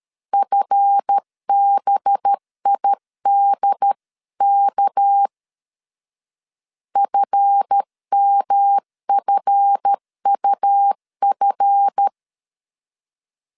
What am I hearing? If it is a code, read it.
Morse code, heard as a keyed tone, about 13 wpm, 800 Hz. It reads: FBIDK FMFUF
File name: LEZIONI / 7 GF9Z / 7 GF 12wpm 2122